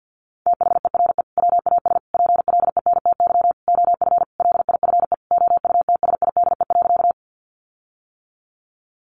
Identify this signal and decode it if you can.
Morse code, heard as a keyed tone, about 50 wpm, 700 Hz. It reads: THERE WAS PLENTY OF LIFE OUTSIDE.
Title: Sentences from Top 1300 Words 50wpm